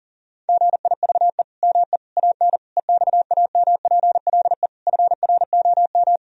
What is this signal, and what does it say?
Morse code, heard as a keyed tone, about 40 wpm, 700 Hz. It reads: GIVE ME AN EXAMPLE FROM
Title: Sentences from Top 1000 Words - Review Part 3 - 40wpm